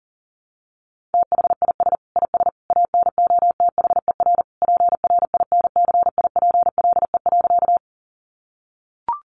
Morse code, heard as a keyed tone, about 37 wpm, 700 Hz. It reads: THIS IS ANOTHER PRINCIPLE.
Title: Sentences from Top 1400 Words 40wpm